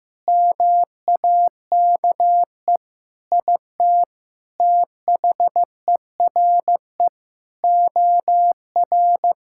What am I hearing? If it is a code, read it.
Morse code, heard as a keyed tone, about 16 wpm, 700 Hz. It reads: MAKE IT THERE OR
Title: Sentences from Top 1600 Words 15wpm